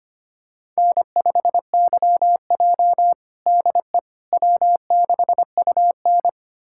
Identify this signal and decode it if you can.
Morse code, heard as a keyed tone, about 25 wpm, 700 Hz. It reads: N5YJ DE W6UN
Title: USA Callsign DE Callsign - 4 Characters - 25wpm